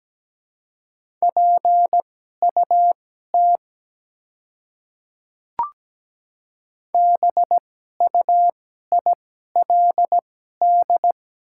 Morse code, heard as a keyed tone, about 17 wpm, spaced as for 12 wpm, 700 Hz. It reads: PUT BUILD
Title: 2x Character Spacing - Top 300 Words - 17wpm